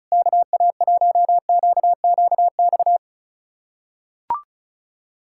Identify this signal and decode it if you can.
Morse code, heard as a keyed tone, about 36 wpm, 700 Hz. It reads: KA1QQX
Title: USA Callsign DE Callsign - 6 Characters - 35wpm